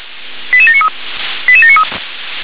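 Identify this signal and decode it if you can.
Five-tone selective call (ZVEI), heard as quick runs of five tones, tones of about 70 ms. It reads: 88793 88793
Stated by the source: Five-tone ZVEI selective call received on SDR.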